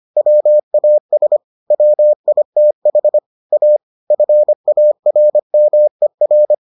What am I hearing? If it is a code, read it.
Morse code, heard as a keyed tone, about 25 wpm, 600 Hz. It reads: WAS WITH A FARMER